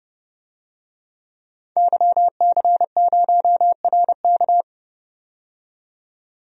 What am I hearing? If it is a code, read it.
Morse code, heard as a keyed tone, about 30 wpm, 700 Hz. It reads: YC0RK
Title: DXCC-100 Call Signs - 5 Characters - 30wpm